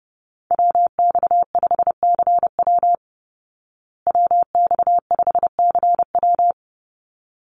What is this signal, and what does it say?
Morse code, heard as a keyed tone, about 30 wpm, 700 Hz. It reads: WX5CW WX5CW
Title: Call Signs Repeated 3x - 30wpm